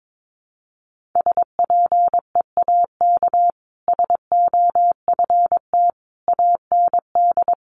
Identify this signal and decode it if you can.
Morse code, heard as a keyed tone, about 22 wpm, 700 Hz. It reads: SPEAK SOFT AND
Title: Sentences from Top 1500 Words 22wpm